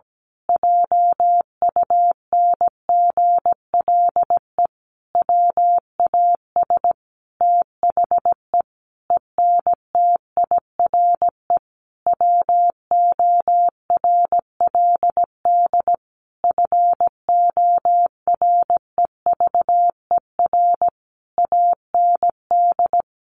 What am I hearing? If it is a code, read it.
Morse code, heard as a keyed tone, about 17 wpm, 700 Hz. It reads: JUNGLE WAS THE ENTIRE WORLD FOREVER AND
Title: A Sound Of Thunder 17wpm